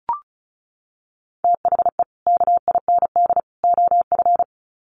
Morse code, heard as a keyed tone, about 34 wpm, 700 Hz. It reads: THE KIND OF
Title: Sets of 3 Words 35wpm